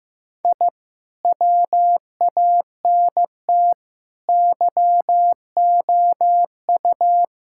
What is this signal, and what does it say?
Morse code, heard as a keyed tone, about 15 wpm, 700 Hz. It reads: I WANT YOU
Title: Sentences from Top 1000 Words - Review Part 6 - 15wpm